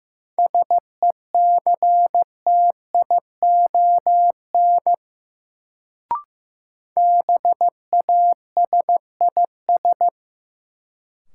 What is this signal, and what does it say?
Morse code, heard as a keyed tone, about 15 wpm, 700 Hz. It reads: SECTION BASIS
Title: Sentences from Top 1000 Words - Review Part 5 - New Word Drill - 15wpm